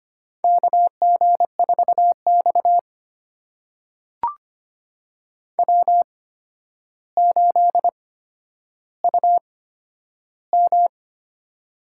Morse code, heard as a keyed tone, about 25 wpm, spaced as for 11 wpm, 700 Hz. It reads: KG4X W8UM
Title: USA Call Signs - 4 Characters - 8x Character Spacing - 25wpm